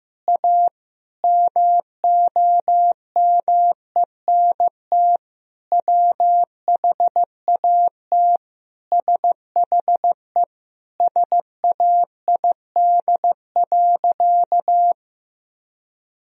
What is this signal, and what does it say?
Morse code, heard as a keyed tone, about 16 wpm, 700 Hz. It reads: A MOMENT WHAT SHE SAID.
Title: Sentences from Top 1000 Words - Review Part 4 - 15wpm